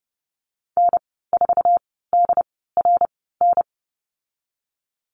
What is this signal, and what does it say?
Morse code, heard as a keyed tone, about 30 wpm, spaced as for 17 wpm, 700 Hz. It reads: N4DRN
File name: Lesson 15g - Number 5 - Call Signs 30wpm